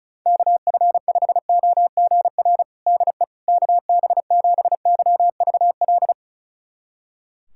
Morse code, heard as a keyed tone, about 36 wpm, 700 Hz. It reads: KF5OGR DE KB7YVL
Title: USA Callsign DE Callsign - 6 Characters - 35wpm